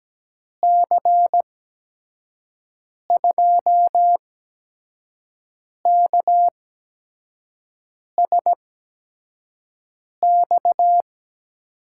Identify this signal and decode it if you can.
Morse code, heard as a keyed tone, about 17 wpm, spaced as for 5 wpm, 700 Hz. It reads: C2KSX
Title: USA Call Signs - 8x Character Spacing - 17wpm